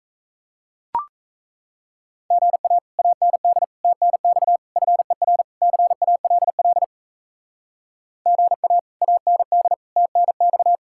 Morse code, heard as a keyed tone, about 36 wpm, 700 Hz. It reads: GA AND TNX FER CALL GA AND TNX